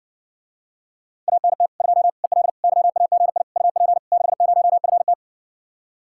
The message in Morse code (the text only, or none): ANT 2 EL YAGI UP 60FT